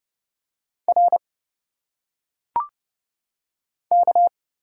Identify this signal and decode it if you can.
Morse code, heard as a keyed tone, about 30 wpm, 700 Hz. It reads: R K
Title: Single Letters - Phonetic Pronunciation 30wpm